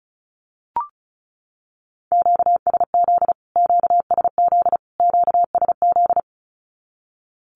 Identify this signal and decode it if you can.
Morse code, heard as a keyed tone, about 34 wpm, 700 Hz. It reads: QSZ QSZ QSZ
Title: Q-Codes Repeated 3x 35wpm